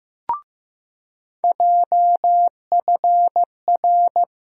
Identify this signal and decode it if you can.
Morse code, heard as a keyed tone, about 15 wpm, 700 Hz. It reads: JFR
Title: USA Call Sign Postfixes - 3 Letters 15wpm